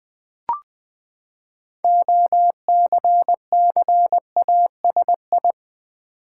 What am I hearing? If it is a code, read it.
Morse code, heard as a keyed tone, about 20 wpm, 700 Hz. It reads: OCCASI
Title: Build A Word - 6 Letters At A Time - 12-Letter Words - 20wpm